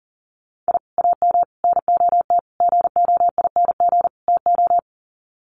Morse code, heard as a keyed tone, about 40 wpm, 700 Hz. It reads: I AM NOT GOING TO
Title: Sets of 5 Words Encore 40wpm